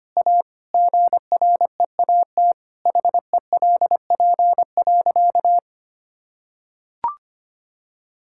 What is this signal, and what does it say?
Morse code, heard as a keyed tone, about 25 wpm, 700 Hz. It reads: A GREAT HELP.
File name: practice / 25wpm / Sentences with 3-Word Phrases - 25wpm